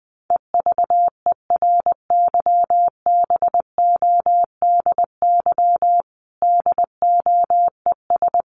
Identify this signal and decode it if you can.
Morse code, heard as a keyed tone, about 20 wpm, 700 Hz. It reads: EVERYBODY DOES
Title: Sentences from Top 1400 Words 20wpm